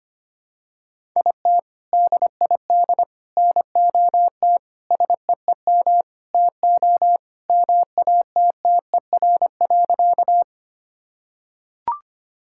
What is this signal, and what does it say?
Morse code, heard as a keyed tone, about 25 wpm, 700 Hz. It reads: IT DID NOT SEEM TO MATTER.
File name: practice / 25wpm / Sentences with 4-Word Phrases - 25wpm